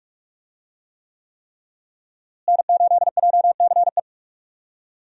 Morse code, heard as a keyed tone, about 47 wpm, 700 Hz. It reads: N8JCE
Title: USA Call Signs - 5 Characters - Repeated 3x - 45wpm